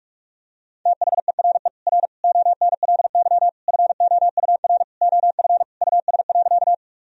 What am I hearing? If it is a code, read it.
Morse code, heard as a keyed tone, about 46 wpm, 700 Hz. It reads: THERE R ONLY FOUR OF US.
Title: Sentences from Top 300 Words and Abbreviations 45wpm